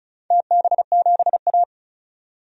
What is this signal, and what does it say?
Morse code, heard as a keyed tone, about 36 wpm, 700 Hz. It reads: TB7A
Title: DXCC-100 Call Signs - 4 Characters - 35wpm